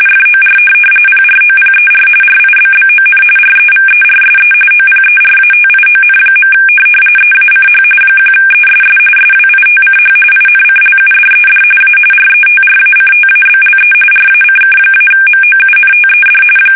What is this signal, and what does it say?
RTTY, about 45.45 baud. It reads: FOR PROPER RIBBON REVERSE OPERATION. EACH 1ESS SWITCH INCLUDED A RETRACTABLE RACK-MOUNTED
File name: fsk850 / teletype-corp